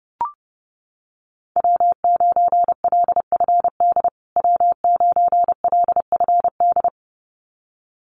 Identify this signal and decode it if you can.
Morse code, heard as a keyed tone, about 30 wpm, 700 Hz. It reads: W9LFD W9LFD
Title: USA Call Signs Repeated 2x 30wpm